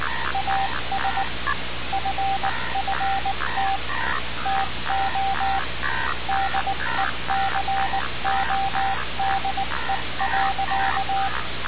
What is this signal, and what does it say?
Morse code, heard as a keyed tone, about 19 wpm, 780 Hz, with noise. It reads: AS FFT TO DECODE.
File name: radiocap / REC z